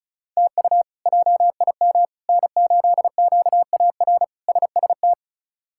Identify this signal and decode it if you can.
Morse code, heard as a keyed tone, about 34 wpm, 700 Hz. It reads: TU JIM N8QAR SST
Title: K1USN SST QSO Practice - 35wpm